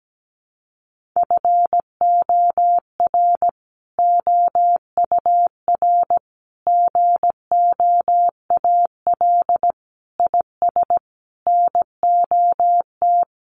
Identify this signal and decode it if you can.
Morse code, heard as a keyed tone, about 17 wpm, 700 Hz. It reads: FOR OUR GOAL IS NOT
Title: Sentences from Top 1400 Words 17wpm